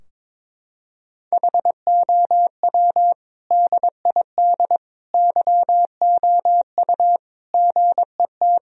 Morse code, heard as a keyed tone, about 22 wpm, 700 Hz. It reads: HOW DID YOU GET
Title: Sentences from Top 1000 Words - Review Part 2 - 22wpm